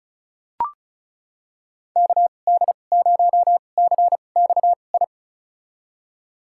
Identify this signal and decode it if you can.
Morse code, heard as a keyed tone, about 34 wpm, spaced as for 25 wpm, 700 Hz. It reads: KD0CXI
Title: USA Call Signs - 2x Character Spacing - 35wpm